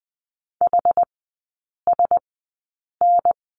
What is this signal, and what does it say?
Morse code, heard as a keyed tone, about 20 wpm, spaced as for 8 wpm, 690 Hz. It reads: HSN
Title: 060 Lesson 11e - Letter h - Three Characters 20wpm